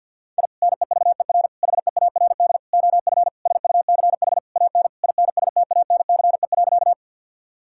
Morse code, heard as a keyed tone, about 47 wpm, 700 Hz. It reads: I NEVER HEARD OF SUCH AN INSTANCE.
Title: Sentences from Top 1000 Words - Review Part 5 - 50wpm